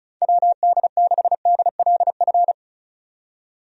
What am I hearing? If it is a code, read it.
Morse code, heard as a keyed tone, about 34 wpm, 700 Hz. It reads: WD6DLF
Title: USA Call Signs - 6 Characters - 35wpm